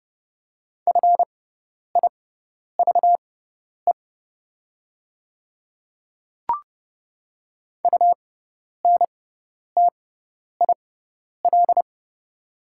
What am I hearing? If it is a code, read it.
Morse code, heard as a keyed tone, about 30 wpm, spaced as for 10 wpm, 700 Hz. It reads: FIVE UNTIL
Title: 6x Character Spacing - Top 500 Words - 30wpm